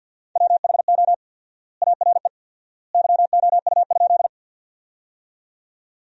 Morse code, heard as a keyed tone, about 50 wpm, 700 Hz. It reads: WHO ARE YOU?
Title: 4x Word-Spacing - Sentences from Top 100 Words - 50wpm